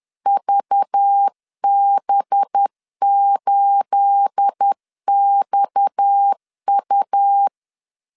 Morse code, heard as a keyed tone, about 10 wpm, 800 Hz. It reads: VB8XU